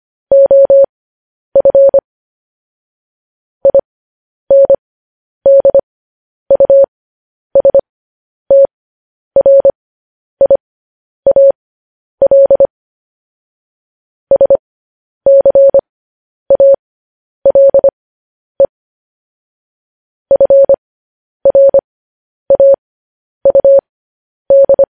Morse code, heard as a keyed tone, about 25 wpm, spaced as for 10 wpm, 550 Hz. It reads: OF INDUSTRIAL SCALE FRAUD